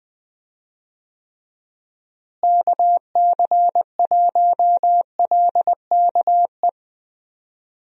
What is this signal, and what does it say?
Morse code, heard as a keyed tone, about 21 wpm, 700 Hz. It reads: KC1LKE
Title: USA Call Signs - 6 Characters - Repeated 3x - 20wpm